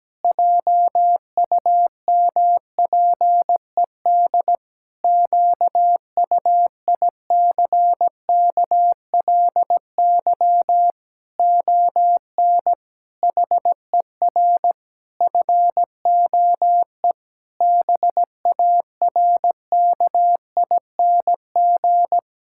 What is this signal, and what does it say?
Morse code, heard as a keyed tone, about 17 wpm, 700 Hz. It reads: JUMPED QUICKLY ON HER FOE BARKING
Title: Pangram Challenge v2 - 17wpm